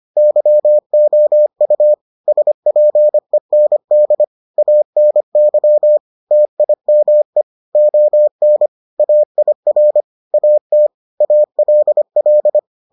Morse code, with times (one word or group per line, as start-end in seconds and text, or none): 0.2-1.9 YOU
2.3-4.2 SPEND
4.6-6.0 ANY
6.3-7.4 TIME
7.8-8.7 ON
9.0-10.0 AIR
10.3-10.9 AT
11.2-12.6 ALL